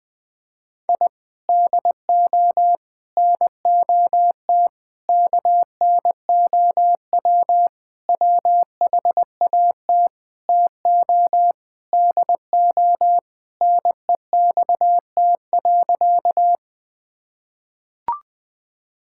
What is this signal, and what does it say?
Morse code, heard as a keyed tone, about 21 wpm, 700 Hz. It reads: I DO NOT KNOW WHAT TO DO NEXT.
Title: Sentences with 5-Word Phrases - 20wpm